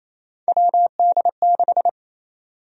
Morse code, heard as a keyed tone, about 28 wpm, 700 Hz. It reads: WD6
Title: USA Call Sign Prefixes - 28wpm